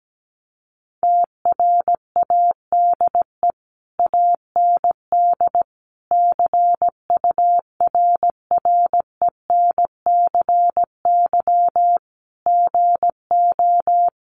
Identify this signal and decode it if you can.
Morse code, heard as a keyed tone, about 17 wpm, 700 Hz. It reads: TRADE AND CURRENCY GO
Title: Sentences from Top 1500 Words 17wpm